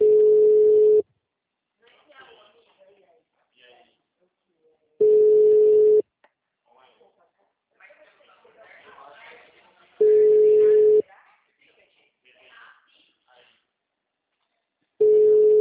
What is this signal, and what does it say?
Ringback tone (Europe, 425 Hz), with speech.